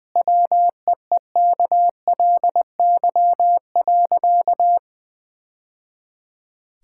Morse code, heard as a keyed tone, about 20 wpm, 700 Hz. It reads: WEEKLY.
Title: Easy Sentences with Hints - 20wpm